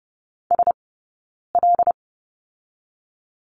Morse code, heard as a keyed tone, about 30 wpm, 700 Hz. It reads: S L
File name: 092 Lesson 16d - Letter c - Two Characters 30wpm